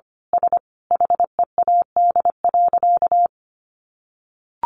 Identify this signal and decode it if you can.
Morse code, heard as a keyed tone, about 25 wpm, 700 Hz. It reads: S HEAD.
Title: Easy Phrases 25wpm